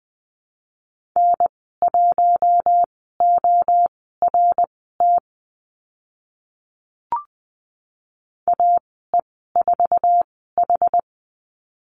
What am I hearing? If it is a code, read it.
Morse code, heard as a keyed tone, about 20 wpm, spaced as for 14 wpm, 700 Hz. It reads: N1ORT AE4H
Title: 076 Lesson 13g - Letter L - Call Signs 20wpm